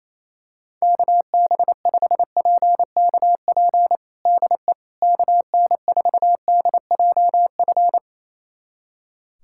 Morse code, heard as a keyed tone, about 28 wpm, 700 Hz. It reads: KB5PKP DE KN4DJF
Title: USA Callsign DE Callsign - 6 Characters - 28wpm